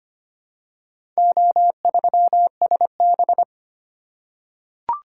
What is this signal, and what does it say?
Morse code, heard as a keyed tone, about 25 wpm, 700 Hz. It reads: O3SB
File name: practice / 25wpm / Four Letter-Number Combinations - 25wpm